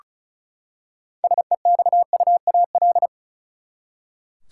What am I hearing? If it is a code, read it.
Morse code, heard as a keyed tone, about 35 wpm, 700 Hz. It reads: SEXUAL
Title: Six Letter Words - 35wpm